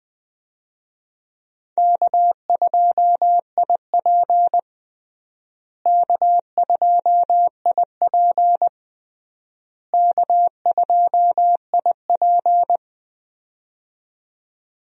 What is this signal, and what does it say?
Morse code, heard as a keyed tone, about 20 wpm, 700 Hz. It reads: K2IP K2IP K2IP
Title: USA Call Signs - 4 Characters - Repeated 3x - 20wpm